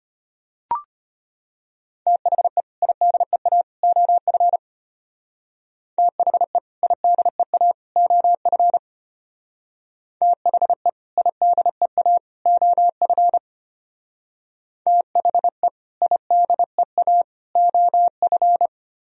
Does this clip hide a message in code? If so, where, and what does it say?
Yes: Morse code from 2.1 to 18.7 s, heard as a keyed tone, about 33 wpm, 700 Hz. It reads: THE IDEA OF THE IDEA OF THE IDEA OF THE IDEA OF